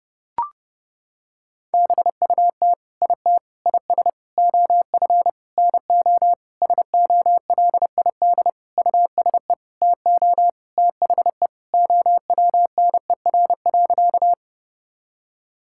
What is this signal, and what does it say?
Morse code, heard as a keyed tone, about 30 wpm, 700 Hz. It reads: BUT IT IS OF NO SOLID USE TO THE OWNER.
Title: Sentences from Top 1600 Words 30wpm